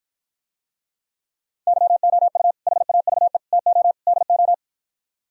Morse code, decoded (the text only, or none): YOU HAVE TO DO